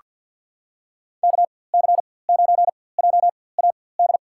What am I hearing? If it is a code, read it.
Morse code, heard as a keyed tone, about 48 wpm, spaced as for 23 wpm, 700 Hz. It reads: KC9JAD
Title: USA Call Signs - 6 Characters - 4x Character Spacing - 50wpm